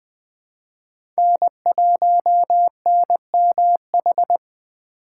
Morse code, heard as a keyed tone, about 20 wpm, 700 Hz. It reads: N1NMH
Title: USA Call Signs - 5 Characters - 2x Character Spacing - 20wpm